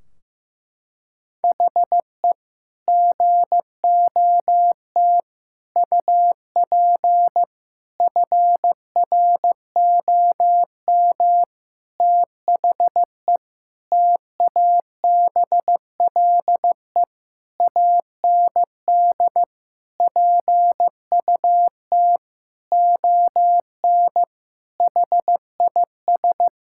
Morse code, heard as a keyed tone, about 15 wpm, 700 Hz. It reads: HE GOT UP FROM THE TABLE AND PUT ON HIS
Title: Sentences from Top 1600 Words 15wpm